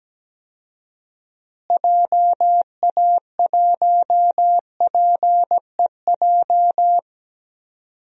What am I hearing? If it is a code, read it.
Morse code, heard as a keyed tone, about 17 wpm, 700 Hz. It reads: JA1PEJ
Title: DXCC-100 Call Signs - 6 Characters - Repeated 3x - 17wpm